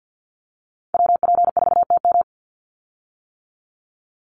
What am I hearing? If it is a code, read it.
Morse code, heard as a keyed tone, about 50 wpm, 700 Hz. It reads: WP4TM
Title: USA Call Signs 50wpm